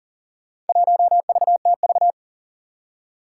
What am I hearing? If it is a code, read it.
Morse code, heard as a keyed tone, about 42 wpm, 700 Hz. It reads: 1VTV
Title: Four Challenging Letter-Number Combinations 40wpm